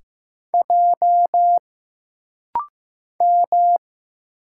Morse code, heard as a keyed tone, about 15 wpm, 700 Hz. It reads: J M
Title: Single Letters - Rapid-Fire 15wpm